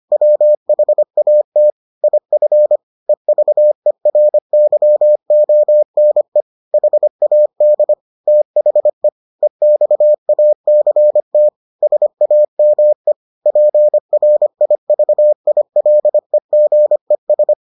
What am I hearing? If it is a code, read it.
Morse code, heard as a keyed tone, about 25 wpm, 600 Hz. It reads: WHAT IF EVERYONE HAD THE EXACT SAME PRIVILEGES